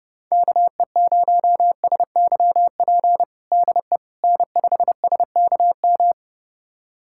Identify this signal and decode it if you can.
Morse code, heard as a keyed tone, about 31 wpm, 700 Hz. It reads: KE0SYP DE N5SKM
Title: USA Callsign DE Callsign 30wpm